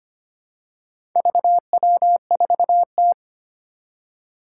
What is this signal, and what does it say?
Morse code, heard as a keyed tone, about 25 wpm, 700 Hz. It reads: VW4T
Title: Four Challenging Letter-Number Combinations 25wpm